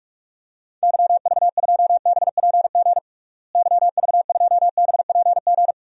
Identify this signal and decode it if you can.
Morse code, heard as a keyed tone, about 42 wpm, 700 Hz. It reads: YV2BPG YV2BPG
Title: DXCC-100 Call Signs - 6 Characters - Repeated 2x - 45wpm